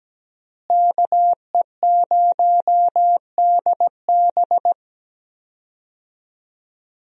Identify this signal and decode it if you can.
Morse code, heard as a keyed tone, about 17 wpm, 700 Hz. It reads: KE0DB
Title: USA Call Signs - 5 Characters - Repeated 3x - 17wpm